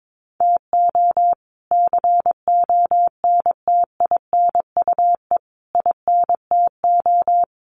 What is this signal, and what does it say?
Morse code, heard as a keyed tone, about 22 wpm, 700 Hz. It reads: TO CONTINUE INTO
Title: Sentences from Top 1500 Words 22wpm